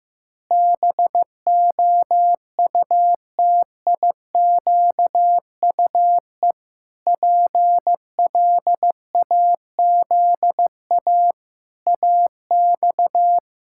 Morse code, heard as a keyed tone, about 16 wpm, 700 Hz. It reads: BOUTIQUE PLAZA AX
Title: Pangram Challenge v2 - 15wpm